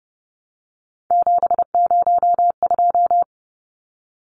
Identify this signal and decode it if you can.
Morse code, heard as a keyed tone, about 30 wpm, 700 Hz. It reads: 702
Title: Numbers - 3-Digit 30wpm